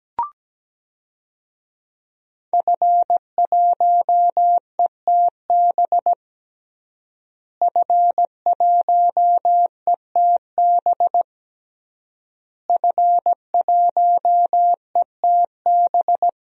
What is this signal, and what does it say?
Morse code, heard as a keyed tone, about 17 wpm, 700 Hz. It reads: F1ETB F1ETB F1ETB
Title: DXCC-100 Call Signs - 5 Characters - Repeated 3x - 17wpm